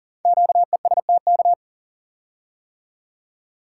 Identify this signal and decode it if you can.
Morse code, heard as a keyed tone, about 38 wpm, 690 Hz. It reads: QESTK
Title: Code Groups - 5 Letters - 40wpm